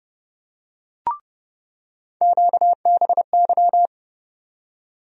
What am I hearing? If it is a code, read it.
Morse code, heard as a keyed tone, about 30 wpm, 700 Hz. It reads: QBY
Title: USA Call Sign Postfixes - 3 Letters 30wpm